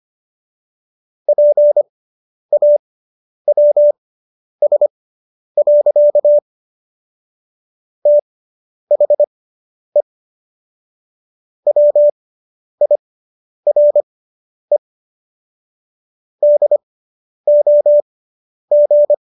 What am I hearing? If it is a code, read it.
Morse code, heard as a keyed tone, about 25 wpm, spaced as for 10 wpm, 590 Hz. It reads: PAWS. THE WIRE DOG